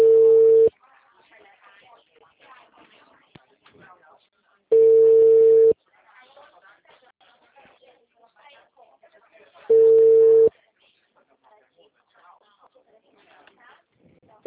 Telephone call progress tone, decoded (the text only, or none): ringback tone